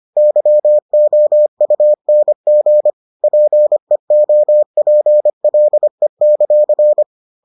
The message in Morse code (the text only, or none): YOUNG PEOPLE;